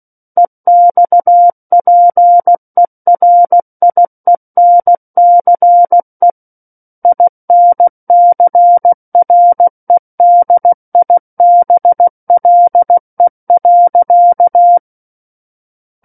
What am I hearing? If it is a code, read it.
Morse code, heard as a keyed tone, about 16 wpm, 700 Hz. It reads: EXPERIENCE INCREDIBLE.
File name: Book One / 07-Chapter 7 - 16 WPM